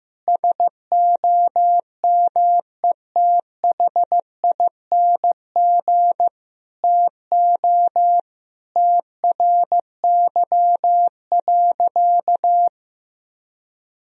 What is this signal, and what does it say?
Morse code, heard as a keyed tone, about 15 wpm, 700 Hz. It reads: SOMETHING TO TRY.
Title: Sentences from Top 1000 Words - Review Part 4 - 15wpm